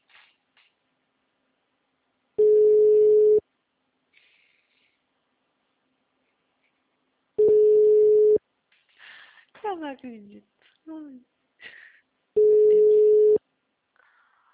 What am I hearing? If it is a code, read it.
Ringback tone (Europe, 425 Hz), with speech.
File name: out- -2024- - - .93343